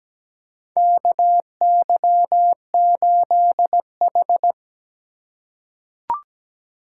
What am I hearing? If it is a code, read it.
Morse code, heard as a keyed tone, about 17 wpm, 700 Hz. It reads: KY8H
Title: USA Call Signs - 4 Characters - 2x Character Spacing - 17wpm